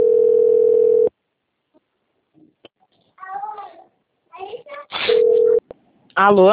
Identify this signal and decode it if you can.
Ringback tone (North America), 440 + 480 Hz, with speech.
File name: out- - - - - .39009